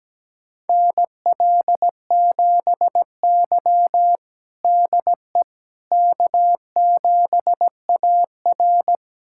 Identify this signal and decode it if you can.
Morse code, heard as a keyed tone, about 17 wpm, 700 Hz. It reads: NL7Y DE K7AR
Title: USA Callsign DE Callsign - 4 Characters - 17wpm